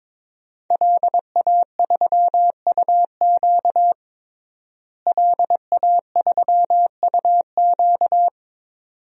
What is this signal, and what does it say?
Morse code, heard as a keyed tone, about 22 wpm, 700 Hz. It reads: LA3UQ LA3UQ
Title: DXCC-100 Call Signs - 5 Characters - Repeated 2x - 22wpm